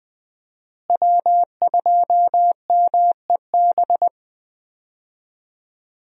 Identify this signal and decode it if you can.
Morse code, heard as a keyed tone, about 20 wpm, 700 Hz. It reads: W2MEB
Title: USA Call Signs - 5 Characters - Repeated 2x - 20wpm